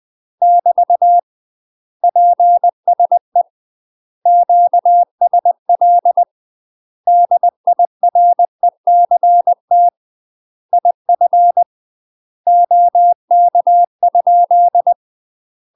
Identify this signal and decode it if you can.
Morse code, heard as a keyed tone, about 20 wpm, 700 Hz. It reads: <BT> PSE QSL DIRECT IF OK ?
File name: UK MASTER QSOs TRAINING / 20WPM / UK MASTER QSOs 0053